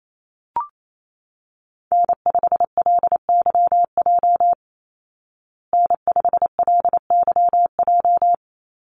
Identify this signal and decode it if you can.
Morse code, heard as a keyed tone, about 28 wpm, 700 Hz. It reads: N5LYJ N5LYJ
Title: Call Signs Repeated 2x - 28wpm